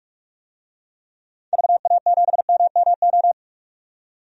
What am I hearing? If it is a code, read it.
Morse code, heard as a keyed tone, about 46 wpm, 700 Hz. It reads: VA7MMO
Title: DXCC-100 Call Signs - 6 Characters - 45wpm